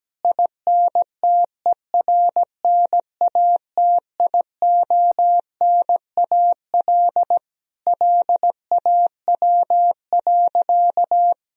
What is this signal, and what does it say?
Morse code, heard as a keyed tone, about 17 wpm, 700 Hz. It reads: INTERNATIONAL LAW.
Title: Sentences from Top 1000 Words - Review Part 4 - 17wpm